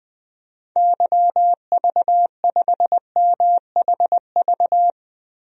Morse code, heard as a keyed tone, about 20 wpm, 700 Hz. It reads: YV5MHV
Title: DXCC-100 Call Signs - 6 Characters - Repeated 3x - 20wpm